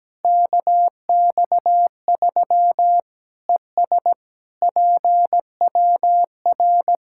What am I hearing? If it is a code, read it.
Morse code, heard as a keyed tone, about 17 wpm, 700 Hz. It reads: KX3 ES PWR
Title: Practice QSO - Exchange 2 - Info - 17wpm